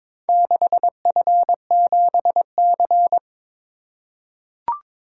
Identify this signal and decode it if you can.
Morse code, heard as a keyed tone, about 22 wpm, 700 Hz. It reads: 6F7C
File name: Four Letter-Number Combinations - 22wpm